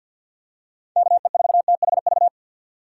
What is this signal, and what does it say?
Morse code, heard as a keyed tone, about 52 wpm, 700 Hz. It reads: KE4THV
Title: USA Call Signs - 6 Characters - 6x Character Spacing - 50wpm